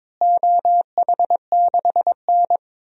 Morse code, heard as a keyed tone, about 22 wpm, 700 Hz. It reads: OH6N
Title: DXCC-100 Call Signs - 4 Characters - Repeated 3x - 22wpm